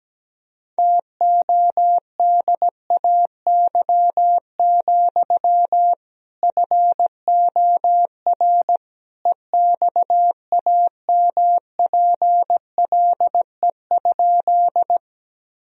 Morse code, heard as a keyed tone, about 17 wpm, 700 Hz. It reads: TODAY, FOR EXAMPLE?
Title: Sentences from Top 1000 Words - Review Part 1 - 17wpm